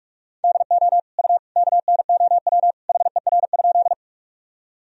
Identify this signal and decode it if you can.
Morse code, heard as a keyed tone, about 46 wpm, 700 Hz. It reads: DO U KNOW HER?
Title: Sentences from Top 100 Words and Abbreviations 45wpm